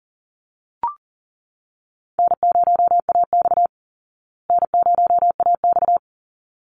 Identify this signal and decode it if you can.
Morse code, heard as a keyed tone, about 38 wpm, 700 Hz. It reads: N0AX N0AX
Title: Call Signs Repeated 3x - 40wpm